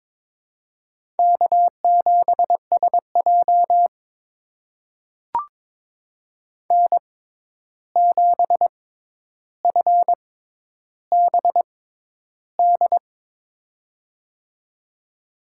Morse code, heard as a keyed tone, about 22 wpm, spaced as for 10 wpm, 700 Hz. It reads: K7SJ N7FBD
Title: USA Call Signs - 6x Character Spacing - 22wpm